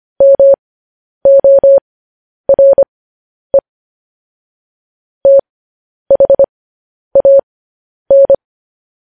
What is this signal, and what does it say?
Morse code, heard as a keyed tone, about 25 wpm, spaced as for 10 wpm, 550 Hz. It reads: MORE THAN
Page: news headlines in morse code at 10 wpm